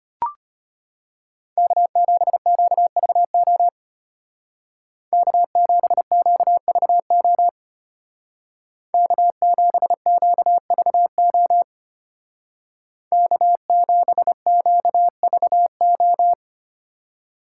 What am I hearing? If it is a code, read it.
Morse code, heard as a keyed tone, about 30 wpm, 700 Hz. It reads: K7QVO K7QVO K7QVO K7QVO
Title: Speed Racing - USA 1x3 Call Signs 25wpm